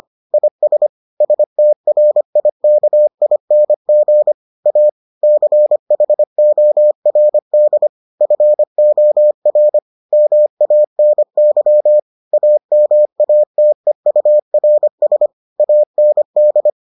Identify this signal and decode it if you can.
Morse code, heard as a keyed tone, about 25 wpm, 600 Hz. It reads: IS STRIKING A CHORD FOR MANY AMATEURS AND